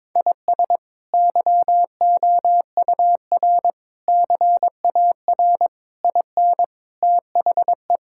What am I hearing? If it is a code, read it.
Morse code, heard as a keyed tone, about 22 wpm, 700 Hz. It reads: IS YOUR CAR IN THE
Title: Sentences from Top 1600 Words 22wpm